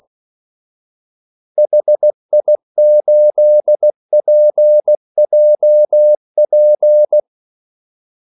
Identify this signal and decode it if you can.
Morse code, heard as a keyed tone, about 16 wpm, 600 Hz. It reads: HI8PJP
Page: Kaldesignaler 16 wpm